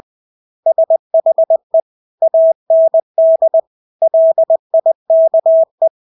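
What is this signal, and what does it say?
Morse code, heard as a keyed tone, about 20 wpm, 650 Hz. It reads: SHE AND LIKE
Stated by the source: Morse practice of English words at 20 wpm.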